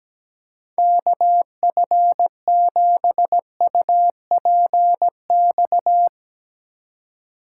Morse code, heard as a keyed tone, about 17 wpm, 700 Hz. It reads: KF7UPX
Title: USA Call Signs - 3x Character Spacing - 17wpm